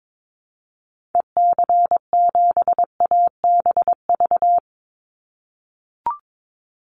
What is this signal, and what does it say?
Morse code, heard as a keyed tone, about 22 wpm, 700 Hz. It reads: EC7ABV
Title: Call Signs - 22wpm